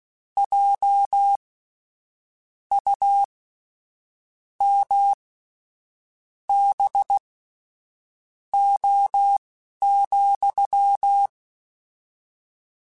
Morse code, heard as a keyed tone, about 16 wpm, spaced as for 6 wpm, 800 Hz. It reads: JUMBO,
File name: Misc / Misc-mp3s / WordPairs sp @10